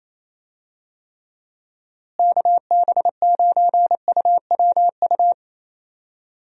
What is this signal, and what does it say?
Morse code, heard as a keyed tone, about 28 wpm, 700 Hz. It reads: KB9UWU
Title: USA Call Signs - 6 Characters - Repeated 2x - 28wpm